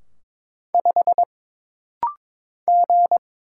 Morse code, heard as a keyed tone, about 22 wpm, 700 Hz. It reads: 5 G
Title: Single Letter-Number Alt - Rapid-Fire 22wpm